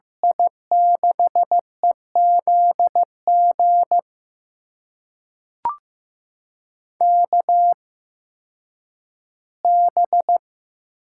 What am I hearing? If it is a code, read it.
Morse code, heard as a keyed tone, about 15 wpm, spaced as for 10 wpm, 700 Hz. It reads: I6EZG KB
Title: USA Call Signs - 6 Characters - 8x Character Spacing - 15wpm